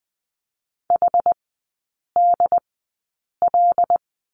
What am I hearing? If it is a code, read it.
Morse code, heard as a keyed tone, about 20 wpm, spaced as for 8 wpm, 700 Hz. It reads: HDL
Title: 074 Lesson 13e - Letter L - Three Characters 20wpm